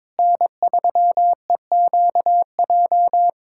Morse code, heard as a keyed tone, about 22 wpm, 700 Hz. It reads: N3EQJ
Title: USA Callsign DE Callsign - 5 Characters - 22wpm